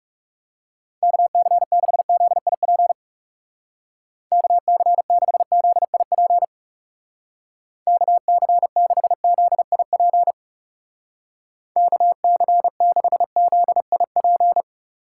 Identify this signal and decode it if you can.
Morse code, heard as a keyed tone, about 38 wpm, 700 Hz. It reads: KC6ZIP KC6ZIP KC6ZIP KC6ZIP